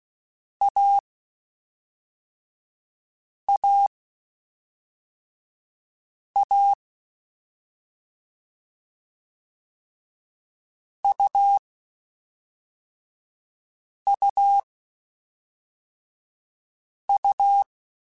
Morse code, heard as a keyed tone, about 16 wpm, 800 Hz. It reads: A A A U U U